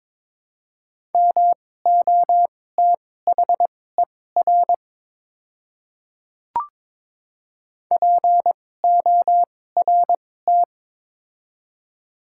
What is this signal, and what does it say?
Morse code, heard as a keyed tone, about 22 wpm, spaced as for 16 wpm, 700 Hz. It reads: MOTHER PORT
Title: 2x Character Spacing - Top 300 Words - 22wpm